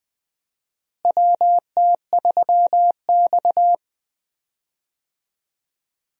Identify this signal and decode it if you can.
Morse code, heard as a keyed tone, about 20 wpm, 700 Hz. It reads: WT3X
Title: USA Call Signs - 4 Characters - Repeated 2x - 20wpm